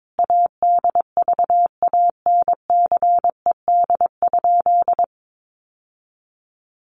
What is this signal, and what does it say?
Morse code, heard as a keyed tone, about 22 wpm, 700 Hz. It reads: ADVANCED?
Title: Sentences from Top 1200 Words 22wpm